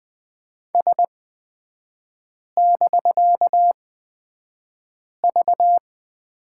Morse code, heard as a keyed tone, about 20 wpm, spaced as for 5 wpm, 700 Hz. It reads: S<BK>V
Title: 250 Lesson 40e - BK - Three Characters 20wpm